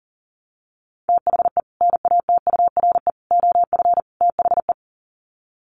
Morse code, heard as a keyed tone, about 40 wpm, 700 Hz. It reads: THE NATURE OF THE
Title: Sets of 4 Words 40wpm